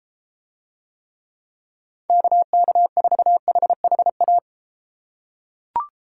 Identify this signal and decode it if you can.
Morse code, heard as a keyed tone, about 34 wpm, 700 Hz. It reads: KK4HHA